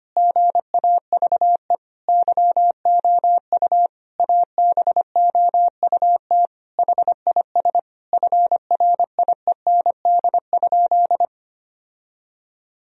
Morse code, heard as a keyed tone, about 25 wpm, 700 Hz. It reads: GAVE YOU ABOUT HIS FRIEND?
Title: Sentences from Top 1000 Words - Review Part 3 - 25wpm